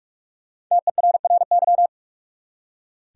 Morse code, heard as a keyed tone, about 45 wpm, 700 Hz. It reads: TERRY
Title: USA Common Names - Max 5 Letters - 45wpm